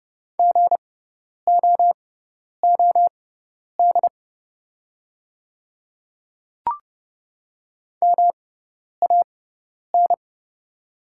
Morse code, heard as a keyed tone, about 31 wpm, spaced as for 10 wpm, 700 Hz. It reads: GOOD MAN
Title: 6x Character Spacing - Top 300 Words - 30wpm